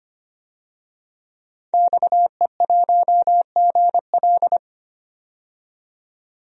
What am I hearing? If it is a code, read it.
Morse code, heard as a keyed tone, about 25 wpm, 700 Hz. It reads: XE1GL
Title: DXCC-100 Call Signs - 5 Characters - 25wpm